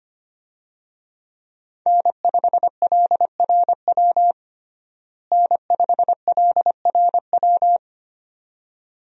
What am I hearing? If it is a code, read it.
Morse code, heard as a keyed tone, about 25 wpm, 700 Hz. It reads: N5LRW N5LRW
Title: USA Call Signs - 5 Characters - Repeated 2x - 25wpm